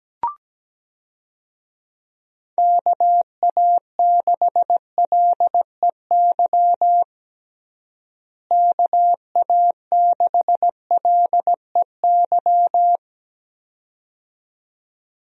Morse code, heard as a keyed tone, about 17 wpm, 700 Hz. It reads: KA6LEY KA6LEY
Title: USA Call Signs - 6 Characters - Repeated 2x - 17wpm